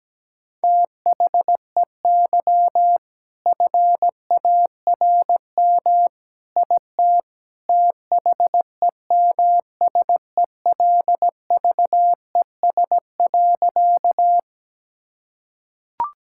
Morse code, heard as a keyed tone, about 17 wpm, 700 Hz. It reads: THEY FARM IT THEMSELVES.
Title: Sentences from Top 1000 Words - Review Part 5 - 17wpm